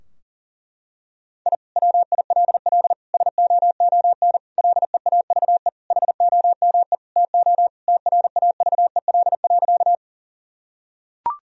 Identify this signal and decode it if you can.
Morse code, heard as a keyed tone, about 40 wpm, 700 Hz. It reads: I WILL SOON LEAVE HOME TO TRAVEL.
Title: Sentences from Top 1000 Words - Review Part 2 - 40wpm